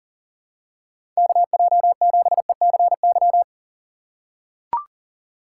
Morse code, heard as a keyed tone, about 40 wpm, 700 Hz. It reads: KJ7ECY
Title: USA Call Signs - 6 Characters - 3x Character Spacing - 40wpm